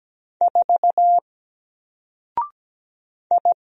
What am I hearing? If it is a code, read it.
Morse code, heard as a keyed tone, about 17 wpm, 700 Hz. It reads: 4 I
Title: Single Character (Letter:Number:Punctuation:Prosign) Alt 17wpm